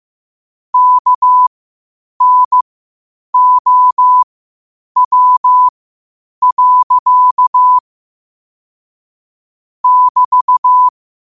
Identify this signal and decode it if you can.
Morse code, heard as a keyed tone, about 15 wpm, spaced as for 8 wpm, 1000 Hz. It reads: KNOW. <BT>